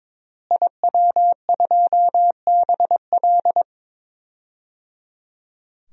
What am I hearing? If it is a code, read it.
Morse code, heard as a keyed tone, about 22 wpm, 700 Hz. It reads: IW2BL
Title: DXCC-100 Call Signs - 5 Characters - Repeated 3x - 22wpm